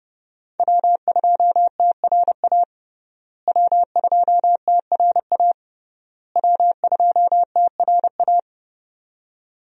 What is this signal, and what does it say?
Morse code, heard as a keyed tone, about 32 wpm, 700 Hz. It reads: W2TRA W2TRA W2TRA
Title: USA Call Signs - 5 Characters - Repeated 3x - 30wpm